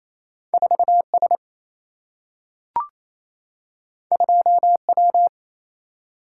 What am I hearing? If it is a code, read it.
Morse code, heard as a keyed tone, about 28 wpm, 700 Hz. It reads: 4 S 2 W
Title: Single-Letter Single-Number 28wpm